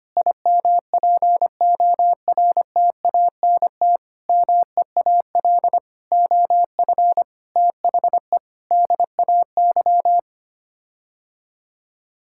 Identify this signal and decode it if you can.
Morse code, heard as a keyed tone, about 25 wpm, 700 Hz. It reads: IMPORTANT MEAL OF THE DAY
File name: Sets of 9 Words - 25wpm